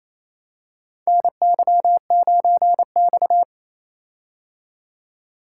Morse code, heard as a keyed tone, about 28 wpm, 700 Hz. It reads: NY9X
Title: K1USN SST QSO Practice - 28wpm